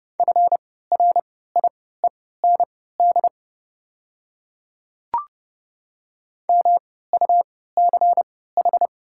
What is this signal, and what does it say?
Morse code, heard as a keyed tone, about 30 wpm, spaced as for 17 wpm, 700 Hz. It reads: FRIEND MUCH
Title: 3x Character Spacing - Top 500 Words - 30wpm